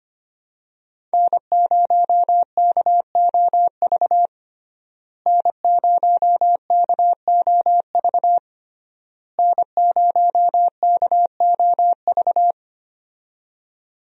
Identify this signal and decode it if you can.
Morse code, heard as a keyed tone, about 25 wpm, 700 Hz. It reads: N0KOV N0KOV N0KOV
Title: USA Call Signs - 5 Characters - Repeated 3x - 25wpm